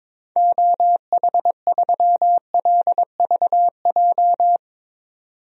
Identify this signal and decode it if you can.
Morse code, heard as a keyed tone, about 22 wpm, 700 Hz. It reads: OH3LVJ
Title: DXCC-100 Call Signs - 6 Characters - 22wpm